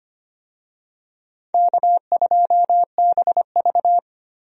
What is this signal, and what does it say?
Morse code, heard as a keyed tone, about 25 wpm, 700 Hz. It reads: K2BV
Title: USA Call Signs - 4 Characters - Repeated 2x - 25wpm